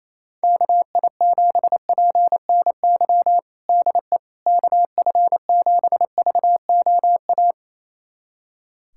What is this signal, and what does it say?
Morse code, heard as a keyed tone, about 28 wpm, 700 Hz. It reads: KI7PNY DE KF7VOA
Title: USA Callsign DE Callsign - 6 Characters - 28wpm